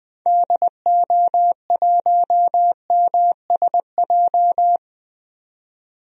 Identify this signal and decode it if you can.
Morse code, heard as a keyed tone, about 20 wpm, 700 Hz. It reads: DO1MSJ
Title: DXCC-100 Call Signs - 6 Characters - 20wpm